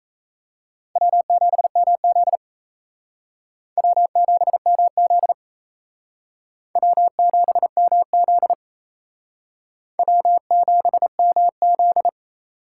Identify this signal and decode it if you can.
Morse code, heard as a keyed tone, about 35 wpm, 700 Hz. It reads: W7MZ W7MZ W7MZ W7MZ
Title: Speed Racing - USA 1x2 Call Signs 28wpm